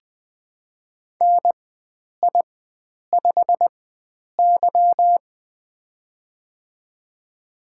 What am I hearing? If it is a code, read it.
Morse code, heard as a keyed tone, about 20 wpm, spaced as for 9 wpm, 700 Hz. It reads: NI5Y
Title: USA Call Signs - 4 Characters - 4x Character Spacing - 20wpm